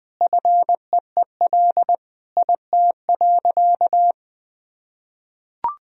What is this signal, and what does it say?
Morse code, heard as a keyed tone, about 21 wpm, 700 Hz. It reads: FEEL IT.
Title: Sentences from Top 400 Words and Abbreviations 20wpm